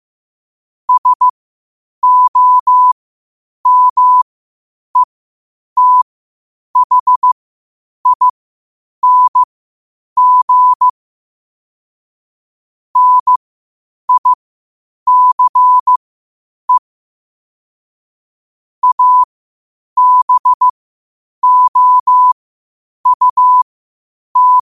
Morse code, heard as a keyed tone, about 15 wpm, spaced as for 8 wpm, 1000 Hz. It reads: SOMETHING NICE ABOUT
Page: Quotes for Fri, 15 Aug 2025 in Morse Code at 8 words per minute.